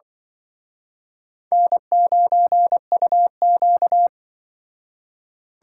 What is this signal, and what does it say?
Morse code, heard as a keyed tone, about 25 wpm, 700 Hz. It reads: N9UQ
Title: Speed Racing - USA 1x2 Call Signs 20wpm